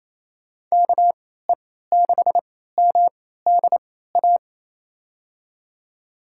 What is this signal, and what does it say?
Morse code, heard as a keyed tone, about 28 wpm, spaced as for 16 wpm, 700 Hz. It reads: KE6MDA
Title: USA Call Signs - 3x Character Spacing - 28wpm